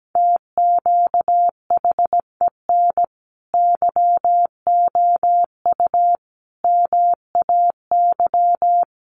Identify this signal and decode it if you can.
Morse code, heard as a keyed tone, about 17 wpm, 700 Hz. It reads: TQHEN YOU MAY
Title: Sentences from Top 100 Words 17wpm